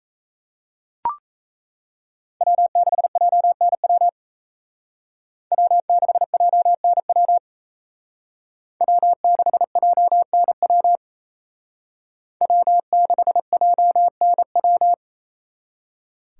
Morse code, heard as a keyed tone, about 36 wpm, 700 Hz. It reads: W6JNW W6JNW W6JNW W6JNW